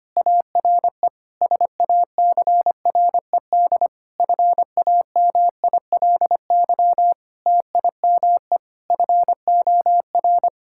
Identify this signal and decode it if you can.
Morse code, heard as a keyed tone, about 25 wpm, 700 Hz. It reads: ARE SACRED FAMILY TIME FOR
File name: practice / 25wpm / Difficult Sentences with Hints - 25wpm